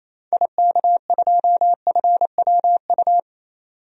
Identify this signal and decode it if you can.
Morse code, heard as a keyed tone, about 28 wpm, 700 Hz. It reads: IK2FWU
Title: DXCC-100 Call Signs - 6 Characters - 28wpm